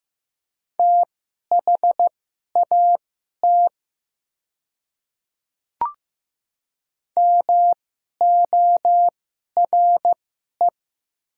Morse code, heard as a keyed tone, about 15 wpm, spaced as for 11 wpm, 690 Hz. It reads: THAT MORE
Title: 2x Character Spacing - Top 100 Words - 15wpm